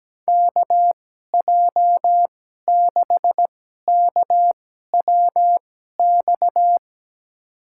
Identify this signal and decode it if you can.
Morse code, heard as a keyed tone, about 17 wpm, spaced as for 12 wpm, 700 Hz. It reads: KJ6KWX
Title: USA Call Signs - 6 Characters - 2x Character Spacing - 17wpm